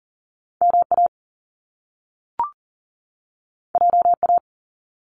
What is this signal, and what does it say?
Morse code, heard as a keyed tone, about 39 wpm, 700 Hz. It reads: MA JA
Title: Top Two Letter Combinations in Words 40wpm